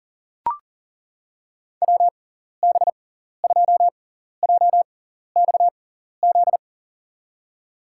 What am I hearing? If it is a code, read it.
Morse code, heard as a keyed tone, about 40 wpm, spaced as for 14 wpm, 700 Hz. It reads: WB2JXZ
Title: USA Call Signs - 6x Character Spacing - 40wpm